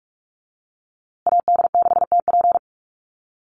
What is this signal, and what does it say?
Morse code, heard as a keyed tone, about 44 wpm, 690 Hz. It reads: AD6TP
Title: USA Call Signs Repeated 3x 45wpm